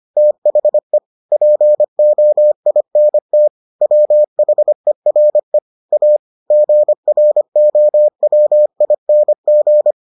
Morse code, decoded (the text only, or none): THE POINT WHERE A GROWING